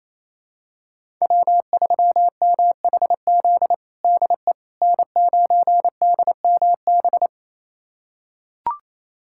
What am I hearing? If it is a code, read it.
Morse code, heard as a keyed tone, about 28 wpm, 700 Hz. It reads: W3MHZ DE N9DMB
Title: USA Callsign DE Callsign - 5 Characters - 28wpm